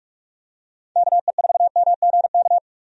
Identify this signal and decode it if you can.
Morse code, heard as a keyed tone, about 46 wpm, 700 Hz. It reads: KE4MGK
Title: USA Call Signs - 6 Characters - 45wpm